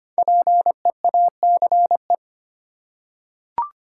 Morse code, heard as a keyed tone, about 25 wpm, 700 Hz. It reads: PEACE
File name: 5 Letter Words - First 3 Letters Given - 25wpm